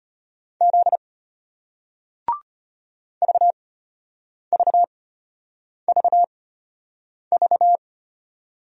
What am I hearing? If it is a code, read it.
Morse code, heard as a keyed tone, about 34 wpm, spaced as for 8 wpm, 700 Hz. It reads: Z VVVV